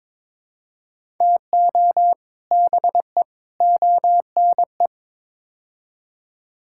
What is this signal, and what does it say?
Morse code, heard as a keyed tone, about 21 wpm, 700 Hz. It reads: TO BE ONE
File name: practice / 22wpm / Build A Phrase - 5-Word Phrases - 22wpm